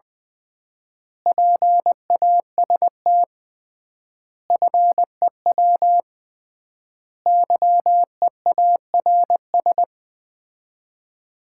Morse code, heard as a keyed tone, about 21 wpm, 700 Hz. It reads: PAST FEW YEARS
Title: 3x Word-Spacing - Sets of 5 Words - 20wpm